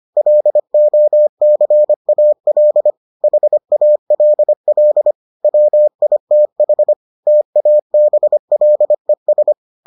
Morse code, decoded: LOCAL HALL WITH TABLES

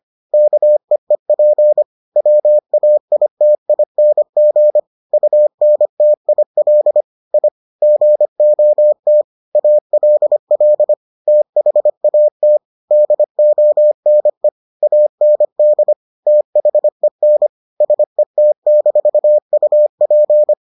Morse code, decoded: KEEP WAITING UNTIL I GOT ALL THAT DONE AND THEN SET-UP

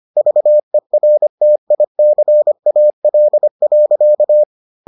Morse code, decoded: VERTICAL.

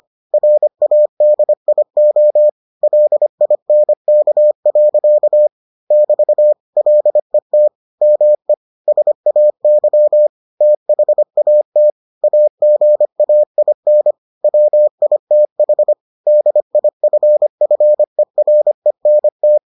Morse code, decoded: RADIO LINK. <BT> LET ME SAY THAT AGAIN WITH DIFFERENT

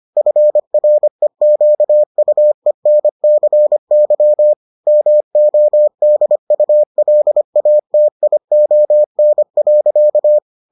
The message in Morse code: FREQUENCY MODULATION.

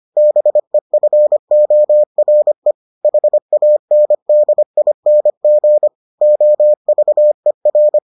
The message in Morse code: BEFORE HANDING OVER

